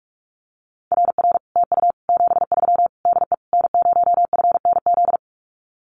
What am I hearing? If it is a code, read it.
Morse code, heard as a keyed tone, about 44 wpm, 700 Hz. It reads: RR TU 73 DE N0FNZ
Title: Common QSO Phrases Encore 45wpm